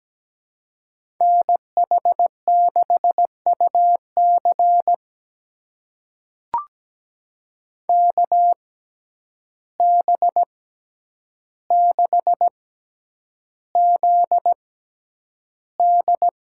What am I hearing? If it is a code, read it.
Morse code, heard as a keyed tone, about 17 wpm, spaced as for 9 wpm, 700 Hz. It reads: NH6UC KB6ZD
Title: USA Call Signs - 5 Characters - 6x Character Spacing - 17wpm